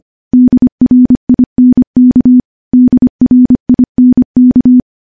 Morse code, heard as a keyed tone, about 25 wpm, 250 Hz. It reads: DRINK DRINK
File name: morse / morse-training-25WPM